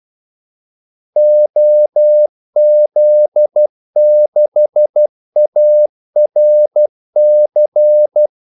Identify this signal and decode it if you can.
Morse code, heard as a keyed tone, about 12 wpm, 600 Hz. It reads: OZ6ARC
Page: 2 ekstra ordmellemrum mellem kaldesignalerne